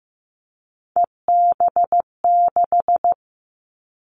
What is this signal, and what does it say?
Morse code, heard as a keyed tone, about 15 wpm, 700 Hz. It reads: EB6
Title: Callsign Prefixes for Top 50 DXCC - 15wpm